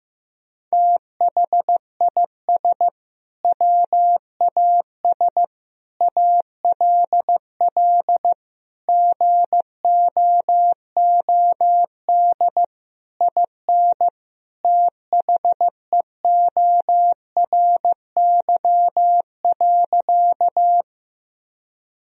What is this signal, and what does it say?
Morse code, heard as a keyed tone, about 16 wpm, 700 Hz. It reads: THIS WAS ALL GOOD IN THEORY.
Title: Sentences from Top 1000 Words - Review Part 3 - 15wpm